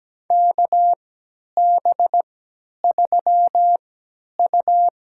Morse code, heard as a keyed tone, about 17 wpm, spaced as for 10 wpm, 700 Hz. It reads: KB3U
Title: USA Call Signs - 6 Characters - 3x Character Spacing - 17wpm